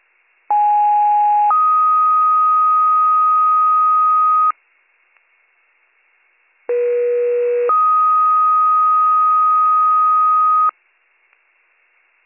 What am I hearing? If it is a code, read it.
Two-tone page, heard as a one-second tone, then a three-second tone, about 830 then 1250 Hz, about 480 then 1250 Hz. It is heard from 0.5 to 10.7 s.